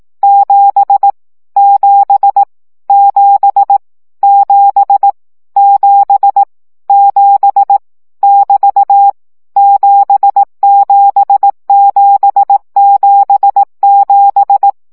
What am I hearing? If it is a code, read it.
Morse code, heard as a keyed tone, about 18 wpm, 800 Hz. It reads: 7 7 7 7 7 7 <BT> 77777
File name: CW gyakorlas 7 15 baud